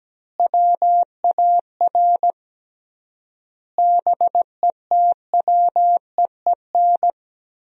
Morse code, heard as a keyed tone, about 17 wpm, 700 Hz. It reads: WAR BETWEEN
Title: 3x Word-Spacing - Sentences from Top 400 Words - 17wpm